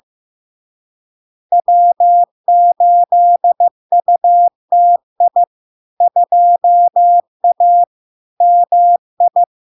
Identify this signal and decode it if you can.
Morse code, heard as a keyed tone, about 15 wpm, 700 Hz. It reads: W8UTI 2A MI
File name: LARK-FD-2022-Log-in-CW-15wpm-700-Hz